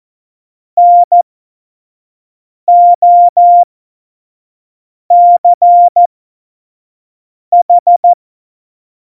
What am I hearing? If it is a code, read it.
Morse code, heard as a keyed tone, about 14 wpm, spaced as for 5 wpm, 700 Hz. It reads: NOCH
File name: Morsekurs / Witze / Witz 11. Dezember 2018, 25 F70